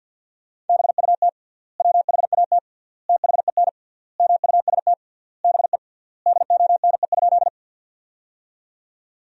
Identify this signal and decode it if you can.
Morse code, heard as a keyed tone, about 52 wpm, 700 Hz. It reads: BUT WHAT THEN MUST BE DONE?
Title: 3x Word-Spacing - Sentences from Top 500 Words - 50wpm